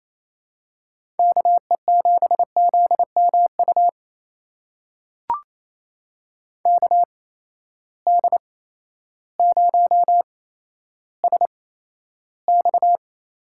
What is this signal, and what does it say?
Morse code, heard as a keyed tone, about 28 wpm, spaced as for 13 wpm, 700 Hz. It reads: KE7ZMU KD0SX
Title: USA Call Signs - 6 Characters - 8x Character Spacing - 28wpm